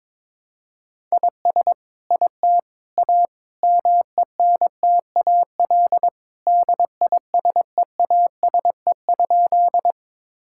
Morse code, heard as a keyed tone, about 22 wpm, 700 Hz. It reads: IS IT A MENTAL DISEASE?
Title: Sentences from Top 1000 Words - Review Part 6 - 22wpm